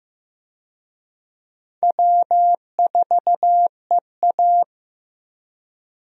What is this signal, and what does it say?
Morse code, heard as a keyed tone, about 15 wpm, 700 Hz. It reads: W4EA
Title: USA Call Signs - 4 Characters - Repeated 3x - 15wpm